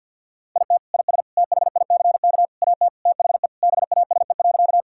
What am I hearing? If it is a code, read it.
Morse code, heard as a keyed tone, about 52 wpm, 700 Hz. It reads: IT IS THICK AT THE BASE.
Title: Sentences from Top 1600 Words 50wpm